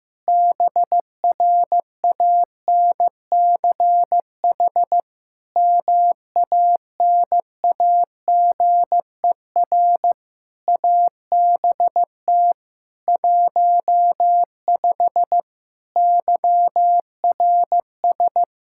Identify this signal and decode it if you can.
Morse code, heard as a keyed tone, about 15 wpm, 700 Hz. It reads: BRANCH MANAGER ABT 15 YRS
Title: Practice QSO - Exchange 3 - More Info - 15wpm